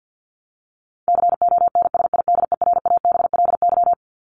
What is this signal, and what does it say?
Morse code, heard as a keyed tone, about 48 wpm, 700 Hz. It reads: CONSIDERABLY
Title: LY Suffix - Top 100 Words with Base Words 50wpm